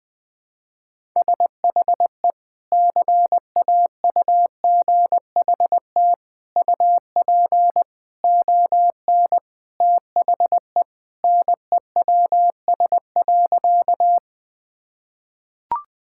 Morse code, heard as a keyed tone, about 20 wpm, 700 Hz. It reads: SHE CAUGHT UP ON THE NEWS.
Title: Sentences with 3-Word Phrases - 20wpm